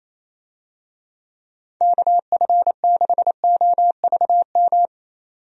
Morse code, heard as a keyed tone, about 28 wpm, 700 Hz. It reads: KF6OVM
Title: USA Call Signs - 6 Characters - Repeated 2x - 28wpm